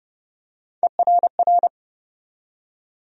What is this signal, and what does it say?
Morse code, heard as a keyed tone, about 30 wpm, 700 Hz. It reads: ERR
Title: USA Call Sign Postfixes - 3 Letters 30wpm